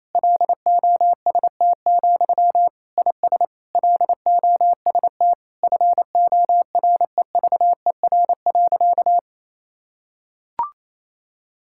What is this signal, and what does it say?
Morse code, heard as a keyed tone, about 28 wpm, 700 Hz. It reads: LOST, IS LOST FOREVER.
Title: Sentences from Top 1600 Words 28wpm